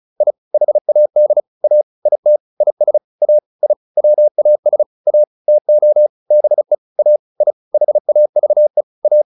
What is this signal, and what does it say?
Morse code, heard as a keyed tone, about 34 wpm, 600 Hz. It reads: I HAD A IT IS A I WAS A TO BE A I HAVE A
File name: WD201 35